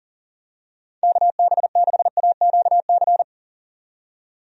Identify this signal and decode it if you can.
Morse code, heard as a keyed tone, about 40 wpm, 700 Hz. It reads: KB6AQC
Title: USA Call Signs - 6 Characters - 40wpm